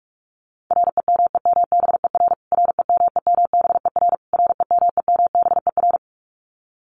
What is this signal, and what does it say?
Morse code, heard as a keyed tone, about 46 wpm, 700 Hz. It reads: REMEMBER REMEMBER REMEMBER
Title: Top 500 Words Repeated 45wpm